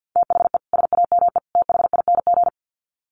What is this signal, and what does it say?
Morse code, heard as a keyed tone, about 50 wpm, 700 Hz. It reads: THE SAME THING
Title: Sets of 3 Words 50wpm